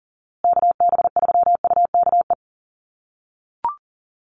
Morse code, heard as a keyed tone, about 40 wpm, 700 Hz. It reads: KB3UKE
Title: USA Call Signs Repeated 2x 40wpm